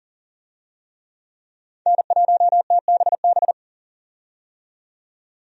Morse code, heard as a keyed tone, about 40 wpm, 700 Hz. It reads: N1TBB
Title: USA Call Signs - 5 Characters - 40wpm